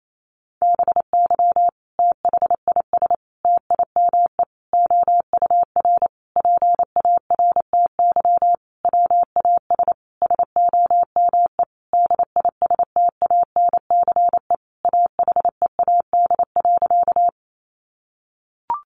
Morse code, heard as a keyed tone, about 28 wpm, 700 Hz. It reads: BY THIS TIME OUR PARTY WAS SOME DISTANCE AHEAD.
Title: Sentences from Top 1400 Words 28wpm